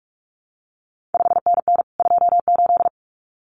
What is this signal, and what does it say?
Morse code, heard as a keyed tone, about 46 wpm, 700 Hz. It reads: 5NN 28
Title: CQ WW DX Contest Exchange Encore - 45wpm